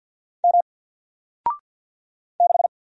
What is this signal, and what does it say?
Morse code, heard as a keyed tone, about 48 wpm, 700 Hz. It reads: M 6
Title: Single Letter-Number Alt 50wpm